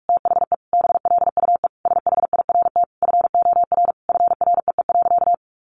Morse code, heard as a keyed tone, about 44 wpm, 700 Hz. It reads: THE BLUE SHIRT FOR FREE.
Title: Additional Easy Phrases 45wpm